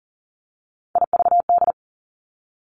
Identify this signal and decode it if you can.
Morse code, heard as a keyed tone, about 40 wpm, 700 Hz. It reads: IVD
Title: Three Challenging Letter-Number Combinations 40wpm